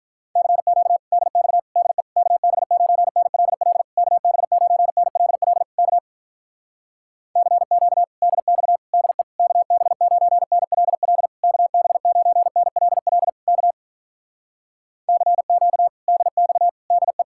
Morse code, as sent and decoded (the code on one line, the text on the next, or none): -.-. --.- / -.. -..- / -.. / -.- -... ----. -. .-.. .-.. / -.- -... ----. -. .-.. .-.. / -.- / -.-. --.- / -.. -..- / -.. / -.- -... ----. -. .-.. .-.. / -.- -... ----. -. .-.. .-.. / -.- / -.-. --.- / -.. -..- / -.. .
CQ DX DE KB9NLL KB9NLL K CQ DX DE KB9NLL KB9NLL K CQ DX DE